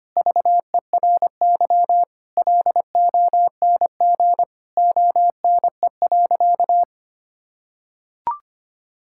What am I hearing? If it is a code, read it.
Morse code, heard as a keyed tone, about 25 wpm, 700 Hz. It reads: VERY LONG ONE.
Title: Sentences from Top 1000 Words - Review Part 5 - 25wpm